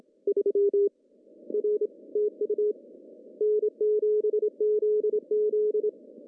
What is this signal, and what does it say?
Morse code, heard as a keyed tone, about 27 wpm, 420 Hz. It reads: 3 R TU N7ZZ
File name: CW / cqww-cw-k5tr